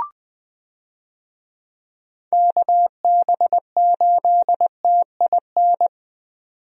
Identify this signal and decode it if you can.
Morse code, heard as a keyed tone, about 20 wpm, 700 Hz. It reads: KB8TIN
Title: USA Call Signs - 6 Characters - 20wpm